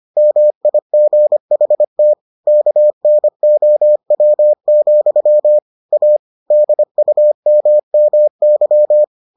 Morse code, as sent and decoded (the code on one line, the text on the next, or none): -- .. --. .... - / -.- -. --- .-- --..-- / .- / -.. ..- -- -- -.--
MIGHT KNOW, A DUMMY